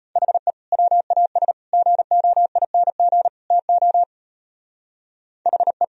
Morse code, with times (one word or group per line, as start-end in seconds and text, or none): 0.2-0.5 HE
0.7-1.5 WAS
1.7-3.3 GOING
3.5-4.0 TO
5.5-5.8 HE